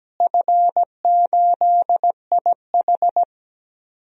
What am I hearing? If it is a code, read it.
Morse code, heard as a keyed tone, about 17 wpm, 700 Hz. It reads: F8IH
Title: DXCC-100 Call Signs - 4 Characters - Repeated 3x - 17wpm